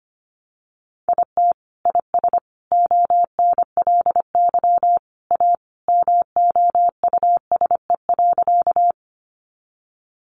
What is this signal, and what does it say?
Morse code, heard as a keyed tone, about 25 wpm, 700 Hz. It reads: IT IS ONLY A MOUSE.
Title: Sentences from Top 700 Words 25wpm